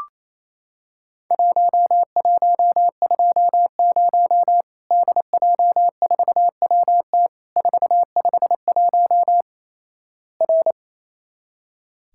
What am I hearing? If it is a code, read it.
Morse code, heard as a keyed tone, about 28 wpm, 700 Hz. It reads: 1120 DJ4WT 451 R
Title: Worked All Europe DX Contest - QTC Exchange - 28wpm